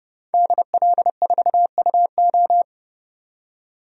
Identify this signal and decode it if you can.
Morse code, heard as a keyed tone, about 30 wpm, 700 Hz. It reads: DL4UO
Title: DXCC-100 Call Signs - 5 Characters - 30wpm